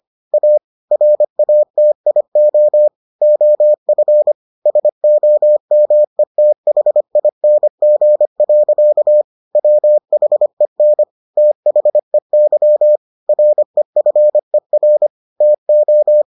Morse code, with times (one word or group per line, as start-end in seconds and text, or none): 0.3-0.6 A
0.9-2.9 RATIO
3.2-4.3 OF
4.7-9.2 SOMETHING.
9.6-11.0 WHEN
11.4-13.0 THEY
13.3-15.1 REFER
15.4-16.2 TO